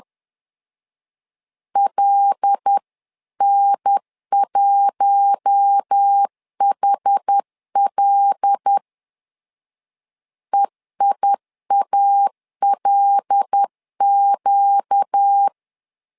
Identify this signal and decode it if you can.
Morse code, heard as a keyed tone, about 10 wpm, 800 Hz. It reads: LN1HL EIALQ
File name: LEZIONI / 5 LBVC / 5 L 10wpm 2259